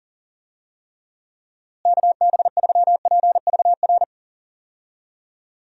Morse code, heard as a keyed tone, about 40 wpm, 700 Hz. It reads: KB3PVR
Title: USA Call Signs - 6 Characters - Repeated 2x - 40wpm